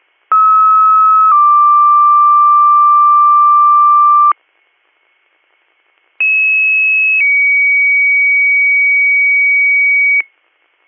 Two-tone page, heard as a one-second tone, then a three-second tone, about 1290 then 1180 Hz, about 2600 then 2250 Hz. It is heard from 0.3 to 10.2 s.